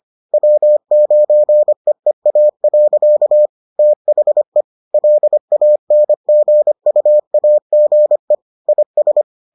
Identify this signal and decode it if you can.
Morse code, heard as a keyed tone, about 25 wpm, 600 Hz. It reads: W9EEA. THE LANGUAGE IS